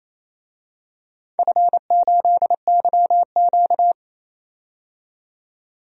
Morse code, heard as a keyed tone, about 28 wpm, 700 Hz. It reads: F8YQ
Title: DXCC-100 Call Signs - 4 Characters - 28wpm